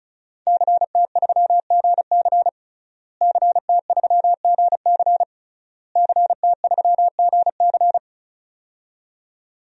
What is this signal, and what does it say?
Morse code, heard as a keyed tone, about 34 wpm, 700 Hz. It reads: CT3GC CT3GC CT3GC
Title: DXCC-100 Call Signs - 5 Characters - Repeated 3x - 35wpm